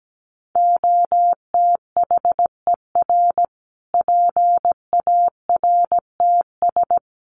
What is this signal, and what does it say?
Morse code, heard as a keyed tone, about 17 wpm, 700 Hz. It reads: OTHER PARTS
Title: Sets of 5 Words Encore 17wpm